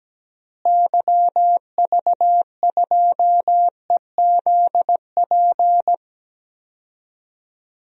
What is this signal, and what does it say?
Morse code, heard as a keyed tone, about 17 wpm, 700 Hz. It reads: YV2EZP
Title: DXCC-100 Call Signs - 6 Characters - Repeated 3x - 17wpm